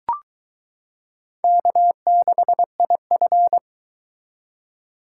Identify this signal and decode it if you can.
Morse code, heard as a keyed tone, about 21 wpm, 700 Hz. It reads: K6IF